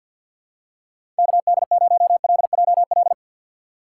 Morse code, heard as a keyed tone, about 53 wpm, 700 Hz. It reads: KD0LJL
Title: USA Call Signs - 6 Characters - 6x Character Spacing - 50wpm